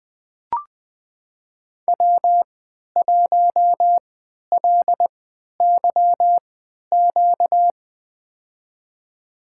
Morse code, heard as a keyed tone, about 20 wpm, spaced as for 11 wpm, 700 Hz. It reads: W1LYQ
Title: USA Call Signs - 3x Character Spacing - 20wpm